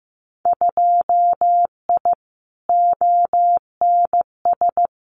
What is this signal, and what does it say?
Morse code, heard as a keyed tone, about 15 wpm, 700 Hz. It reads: 2I ONS
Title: Winter Field Day Exchanges Encore 15wpm